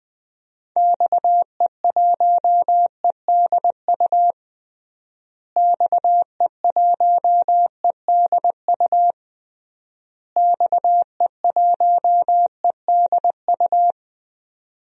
Morse code, heard as a keyed tone, about 20 wpm, 700 Hz. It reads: XE1EDU XE1EDU XE1EDU
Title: DXCC-100 Call Signs - 6 Characters - Repeated 3x - 20wpm